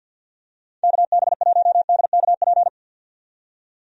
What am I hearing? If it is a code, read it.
Morse code, heard as a keyed tone, about 52 wpm, 700 Hz. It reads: KB1DKP
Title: USA Call Signs - 6 Characters - 50wpm